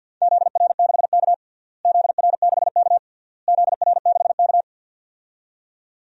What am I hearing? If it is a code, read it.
Morse code, heard as a keyed tone, about 51 wpm, 700 Hz. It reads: ZR6K ZR6K ZR6K
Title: DXCC-100 Call Signs - 4 Characters - Repeated 3x - 50wpm